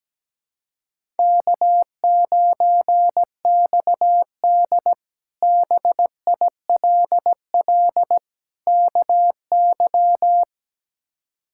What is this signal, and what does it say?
Morse code, heard as a keyed tone, about 17 wpm, 700 Hz. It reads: K9XD BILL KY
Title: K1USN SST QSO Practice - 17wpm